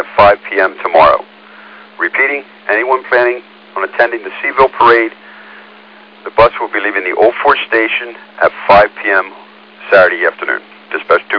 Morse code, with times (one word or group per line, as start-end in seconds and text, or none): none